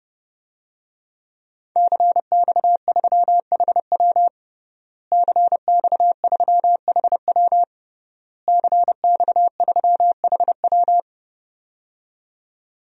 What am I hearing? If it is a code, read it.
Morse code, heard as a keyed tone, about 32 wpm, 700 Hz. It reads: CX3HW CX3HW CX3HW
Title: DXCC-100 Call Signs - 5 Characters - Repeated 3x - 30wpm